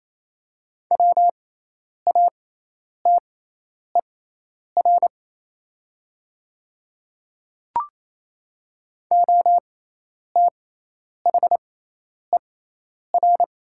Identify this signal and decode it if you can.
Morse code, heard as a keyed tone, about 27 wpm, spaced as for 10 wpm, 690 Hz. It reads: WATER OTHER
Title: 6x Character Spacing - Top 100 Words - 28wpm